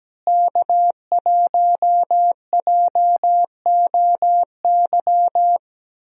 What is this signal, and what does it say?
Morse code, heard as a keyed tone, about 17 wpm, 700 Hz. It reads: K1JOY
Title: USA Call Signs - 5 Characters - Repeated 2x - 17wpm